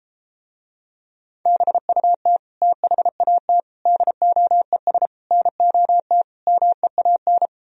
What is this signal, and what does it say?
Morse code, heard as a keyed tone, about 33 wpm, 700 Hz. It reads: BUT THAT DOES NOT MEAN